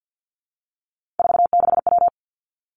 Morse code, heard as a keyed tone, about 50 wpm, 700 Hz. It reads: 46W
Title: Three Challenging Letter-Number Combinations 50wpm